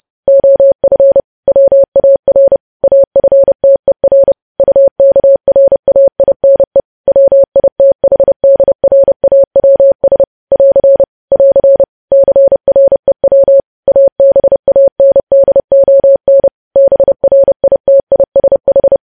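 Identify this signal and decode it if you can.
Morse code, heard as a keyed tone, about 30 wpm, 550 Hz. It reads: OF WAR AFTER UKRAINE WITHDRAWS <AR> <AR> CREW ABANDON BRITISH
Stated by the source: News headlines sent in Morse at 30.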